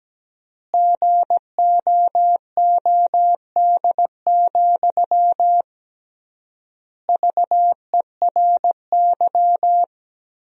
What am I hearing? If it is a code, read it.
Morse code, heard as a keyed tone, about 17 wpm, 700 Hz. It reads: GOOD, VERY
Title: 3x Word-Spacing - Sentences from Top 200 Words - 17wpm